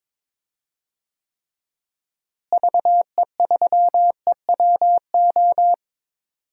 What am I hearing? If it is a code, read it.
Morse code, heard as a keyed tone, about 22 wpm, 700 Hz. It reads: VE3EWO
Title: DXCC-100 Call Signs - 6 Characters - Repeated 3x - 22wpm